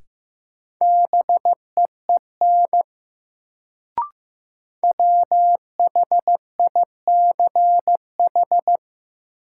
Morse code, heard as a keyed tone, about 15 wpm, 700 Hz. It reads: BEEN WHICH
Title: Sentences from Top 100 Words - New Word Drill - 15wpm